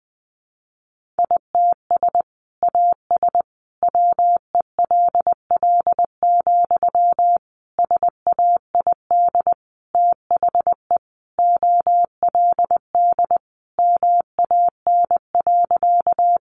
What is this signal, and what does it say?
Morse code, heard as a keyed tone, about 20 wpm, 700 Hz. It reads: ITS AS WELL, SAID THE OLD MAN.
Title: Sentences from Top 600 Words 20wpm